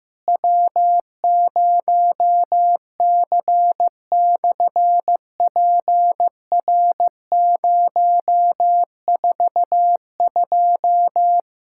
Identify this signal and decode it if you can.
Morse code, heard as a keyed tone, about 15 wpm, 700 Hz. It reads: W0C/PR042
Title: SOTA QSO Practice 15wpm